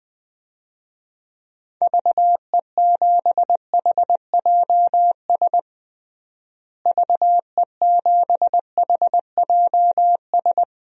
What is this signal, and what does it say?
Morse code, heard as a keyed tone, about 20 wpm, 700 Hz. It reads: VE7HJS VE7HJS
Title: DXCC-100 Call Signs - 6 Characters - Repeated 2x - 20wpm